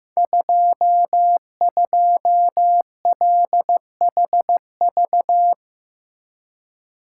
Morse code, heard as a keyed tone, about 15 wpm, 700 Hz. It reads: 22LHV
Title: Code Groups - 5 Letter:Numbers - 15wpm